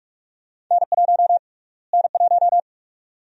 Morse code, heard as a keyed tone, about 42 wpm, 700 Hz. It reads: N1 N1
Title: USA Call Sign Prefixes - Repeated 2x - 45wpm